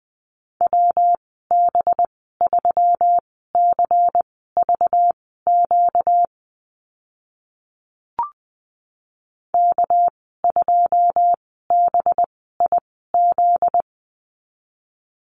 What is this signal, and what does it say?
Morse code, heard as a keyed tone, about 20 wpm, spaced as for 14 wpm, 700 Hz. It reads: WB3CVQ K2BIZ
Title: 245 Lesson 39g - Letter z - Call Signs 20wpm